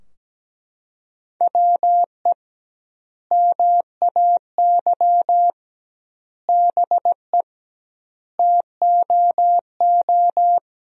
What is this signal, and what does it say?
Morse code, heard as a keyed tone, about 17 wpm, 700 Hz. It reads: WE MAY BE TOO
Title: 2x Word-Spacing - Sentences from Top 300 Words - 17wpm